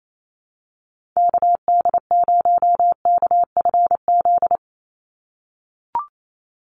Morse code, heard as a keyed tone, about 28 wpm, 700 Hz. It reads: KD0KFZ
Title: USA Call Signs 28wpm